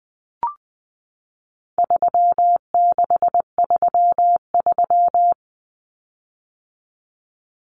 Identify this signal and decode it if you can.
Morse code, heard as a keyed tone, about 20 wpm, 700 Hz. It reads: 3633
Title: Common Cut Numbers - 4-Digit 20wpm